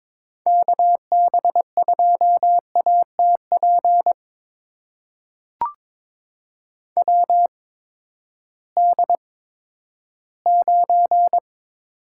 Morse code, heard as a keyed tone, about 22 wpm, spaced as for 13 wpm, 700 Hz. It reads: KB2ATP WD9
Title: USA Call Signs - 6 Characters - 8x Character Spacing - 22wpm